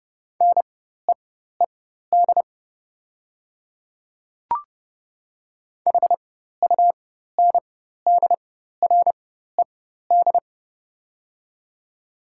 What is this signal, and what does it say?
Morse code, heard as a keyed tone, about 30 wpm, spaced as for 14 wpm, 700 Hz. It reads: NEED HUNDRED
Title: 4x Character Spacing - Top 500 Words - 30wpm